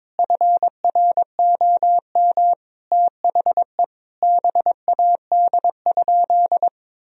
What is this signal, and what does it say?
Morse code, heard as a keyed tone, about 22 wpm, 700 Hz. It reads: FROM THE BAD?
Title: Sentences from Top 1000 Words - Review Part 1 - 22wpm